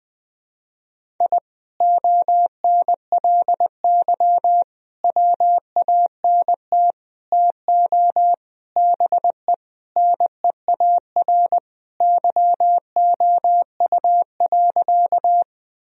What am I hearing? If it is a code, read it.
Morse code, heard as a keyed tone, about 21 wpm, 700 Hz. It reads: I ONLY WANT TO BE NEAR YOU.
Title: Sentences from Top 1000 Words - Review Part 1 - 20wpm